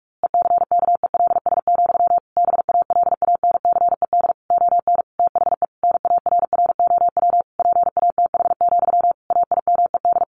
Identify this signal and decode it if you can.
Morse code, heard as a keyed tone, about 44 wpm, 700 Hz. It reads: ECKELS, BALANCED ON THE NARROW PATH, AIMED
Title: A Sound Of Thunder 45wpm